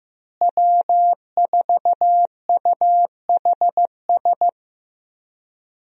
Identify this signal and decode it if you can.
Morse code, heard as a keyed tone, about 15 wpm, 700 Hz. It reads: W4UHS
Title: USA Call Signs - 5 Characters - Repeated 2x - 15wpm